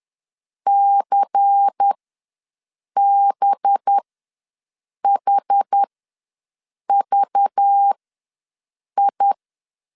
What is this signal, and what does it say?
Morse code, heard as a keyed tone, about 10 wpm, spaced as for 6 wpm, 800 Hz. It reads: CBHVI
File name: LEZIONI / 5 LBVC / 5 LBVC ws 10wpm 3733